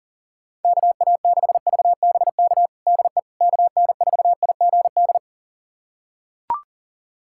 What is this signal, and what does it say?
Morse code, heard as a keyed tone, about 40 wpm, 700 Hz. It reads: KA6VBK DE KN4IGD
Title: USA Callsign DE Callsign 40wpm